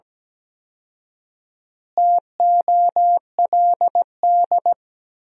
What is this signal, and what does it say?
Morse code, heard as a keyed tone, about 17 wpm, 700 Hz. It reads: TOLD
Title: 4x Word-Spacing - Sentences from Top 400 Words - 17wpm